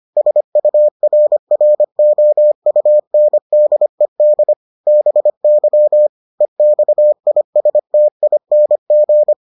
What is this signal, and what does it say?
Morse code, heard as a keyed tone, about 25 wpm, 600 Hz. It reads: SURROUNDED BY EXISTING